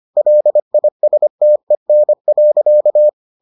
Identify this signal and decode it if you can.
Morse code, heard as a keyed tone, about 25 wpm, 600 Hz. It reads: LISTEN.